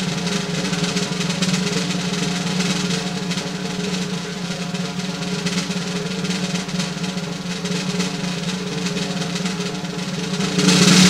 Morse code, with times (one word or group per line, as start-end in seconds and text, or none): none